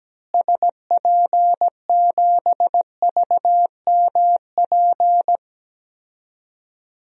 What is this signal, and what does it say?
Morse code, heard as a keyed tone, about 17 wpm, 700 Hz. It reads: SP7VMP
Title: DXCC-100 Call Signs - 6 Characters - Repeated 3x - 17wpm